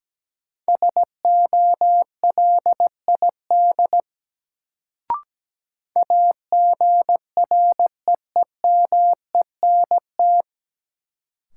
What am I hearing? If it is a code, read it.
Morse code, heard as a keyed tone, about 17 wpm, 700 Hz. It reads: SOLID AGREEMENT
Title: Sentences from Top 1500 Words - New Word Drill - 17wpm